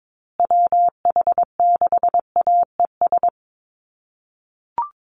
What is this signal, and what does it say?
Morse code, heard as a keyed tone, about 22 wpm, 700 Hz. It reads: WH6AES
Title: USA Call Signs Repeated 3x 22wpm